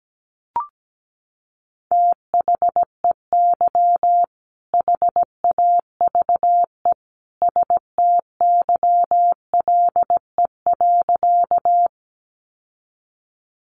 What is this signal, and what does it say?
Morse code, heard as a keyed tone, about 17 wpm, 700 Hz. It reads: THEY HAVE STYLE.
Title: Sentences from Top 800 Words 17wpm